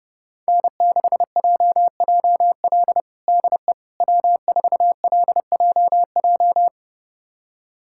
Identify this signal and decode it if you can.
Morse code, heard as a keyed tone, about 30 wpm, 700 Hz. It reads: N6JJL DE W4LJJ
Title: USA Callsign DE Callsign - 5 Characters - 30wpm